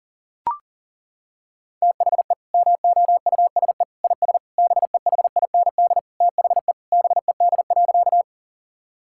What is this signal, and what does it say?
Morse code, heard as a keyed tone, about 40 wpm, 700 Hz. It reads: THE MOUSE IS BEHIND THE BED.
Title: Sentences from Top 1000 Words - Review Part 5 - 40wpm